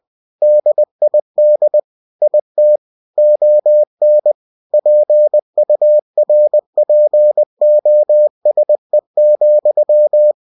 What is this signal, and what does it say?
Morse code, heard as a keyed tone, about 20 wpm, 600 Hz. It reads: DID IT ON PURPOSE,
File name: cw-books / winnie-the-pooh / winnie-20wpm-page-0072